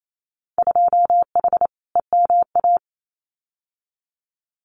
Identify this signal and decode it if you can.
Morse code, heard as a keyed tone, about 28 wpm, 700 Hz. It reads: 2H EMA
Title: Winter Field Day Exchanges 28wpm